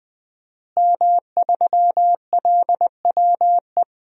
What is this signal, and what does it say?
Morse code, heard as a keyed tone, about 20 wpm, 700 Hz. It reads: M3LWE
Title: DXCC-100 Call Signs - 5 Characters - Repeated 2x - 20wpm